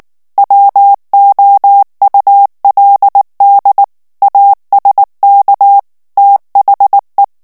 Morse code, heard as a keyed tone, about 19 wpm, 800 Hz. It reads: WOULD ASK THE
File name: ANGLAIS texte clair 3 vitesse 18 mots minute